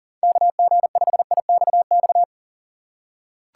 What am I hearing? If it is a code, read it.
Morse code, heard as a keyed tone, about 40 wpm, 700 Hz. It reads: KG5IXX
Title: USA Call Signs - 6x Character Spacing - 40wpm